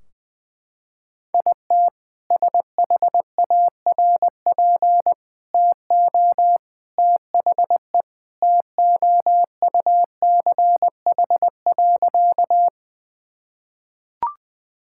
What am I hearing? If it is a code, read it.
Morse code, heard as a keyed tone, about 20 wpm, 700 Hz. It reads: IT SHARP TO THE TOUCH.
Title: Sentences from Top 1600 Words 20wpm